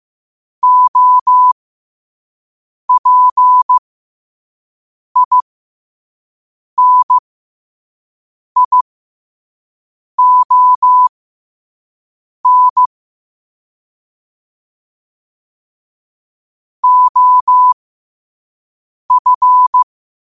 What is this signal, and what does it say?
Morse code, heard as a keyed tone, about 15 wpm, spaced as for 5 wpm, 1000 Hz. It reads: OPINION OF
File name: qotdfiles / 5wpm